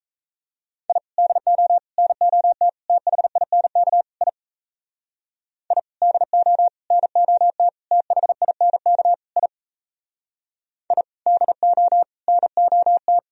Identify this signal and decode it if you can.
Morse code, heard as a keyed tone, about 33 wpm, 700 Hz. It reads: I DO NOT THINK I I DO NOT THINK I I DO NOT
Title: Speed-Racing - Sets of 5 Words 28wpm